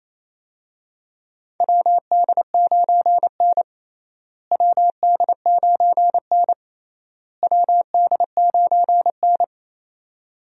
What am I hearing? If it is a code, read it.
Morse code, heard as a keyed tone, about 28 wpm, 700 Hz. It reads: WD9N WD9N WD9N
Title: USA Call Signs - 4 Characters - Repeated 3x - 28wpm